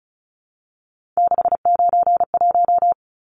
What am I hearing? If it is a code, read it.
Morse code, heard as a keyed tone, about 34 wpm, 700 Hz. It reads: 691
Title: Numbers - 3-Digit 35wpm